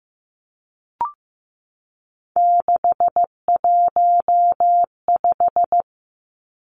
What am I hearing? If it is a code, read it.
Morse code, heard as a keyed tone, about 15 wpm, 700 Hz. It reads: 615
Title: Three Challenging Letter-Number Combinations 15wpm